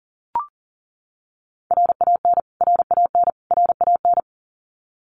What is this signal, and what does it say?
Morse code, heard as a keyed tone, about 40 wpm, 700 Hz. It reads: RAN RAN RAN
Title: Top 500 Words Repeated 40wpm